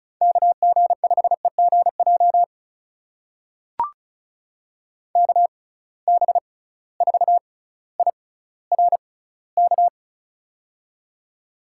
Morse code, heard as a keyed tone, about 36 wpm, spaced as for 18 wpm, 700 Hz. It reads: KG5EGJ KB4IRK
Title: USA Call Signs - 6 Characters - 6x Character Spacing - 35wpm